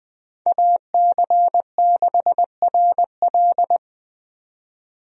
Morse code, heard as a keyed tone, about 21 wpm, 700 Hz. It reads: AC6RL
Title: USA Call Signs - 5 Characters - 20wpm